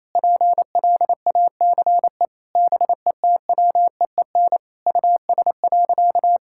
Morse code, heard as a keyed tone, about 28 wpm, 700 Hz. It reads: PLACE BETWEEN US.
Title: Sentences from Top 1000 Words - Review Part 6 - 28wpm